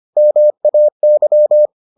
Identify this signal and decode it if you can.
Morse code, heard as a keyed tone, about 25 wpm, 600 Hz. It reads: MAY